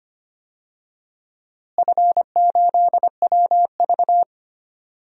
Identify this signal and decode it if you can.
Morse code, heard as a keyed tone, about 25 wpm, 700 Hz. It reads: F8WV
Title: DXCC-100 Call Signs - 4 Characters - Repeated 3x - 25wpm